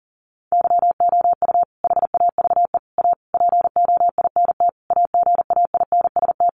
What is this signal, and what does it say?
Morse code, heard as a keyed tone, about 38 wpm, 700 Hz. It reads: YOU HAVE A POINT AGAINST
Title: Sentences from Top 1400 Words 40wpm